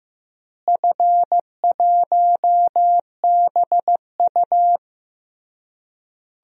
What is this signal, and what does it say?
Morse code, heard as a keyed tone, about 15 wpm, 700 Hz. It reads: F1BU
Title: DXCC-100 Call Signs - 4 Characters - Repeated 3x - 15wpm